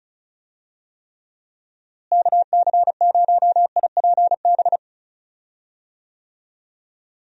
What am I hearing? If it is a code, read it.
Morse code, heard as a keyed tone, about 36 wpm, 700 Hz. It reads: KC0IPB
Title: USA Call Signs - 6 Characters - Repeated 2x - 35wpm